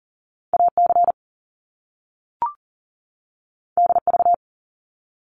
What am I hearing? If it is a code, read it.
Morse code, heard as a keyed tone, about 39 wpm, 700 Hz. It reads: AC DV
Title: Top Two Letter Combinations in Words 40wpm